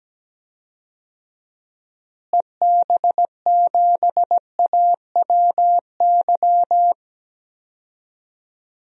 Morse code, heard as a keyed tone, about 17 wpm, 700 Hz. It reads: EB7AWY
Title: DXCC-100 Call Signs - 6 Characters - Repeated 2x - 17wpm